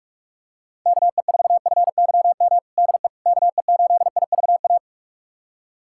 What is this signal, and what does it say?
Morse code, heard as a keyed tone, about 46 wpm, 700 Hz. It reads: KE4FYM DE KE8IVA
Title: USA Callsign DE Callsign - 6 Characters - 45wpm